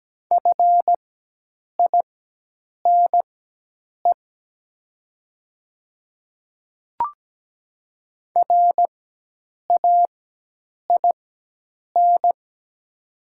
Morse code, heard as a keyed tone, about 17 wpm, spaced as for 8 wpm, 700 Hz. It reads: FINE RAIN
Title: 4x Character Spacing - Top 500 Words - 17wpm